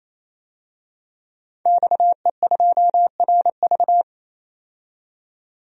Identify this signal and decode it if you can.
Morse code, heard as a keyed tone, about 28 wpm, 700 Hz. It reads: XE2RV
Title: DXCC-100 Call Signs - 5 Characters - 28wpm